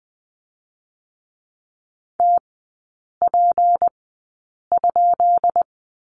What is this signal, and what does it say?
Morse code, heard as a keyed tone, about 20 wpm, spaced as for 8 wpm, 700 Hz. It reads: TP?
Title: 191 Lesson 31e - Letter v - Three Characters 20wpm